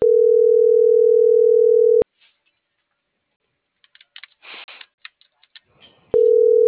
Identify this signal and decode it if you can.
Ringback tone (North America), 440 + 480 Hz.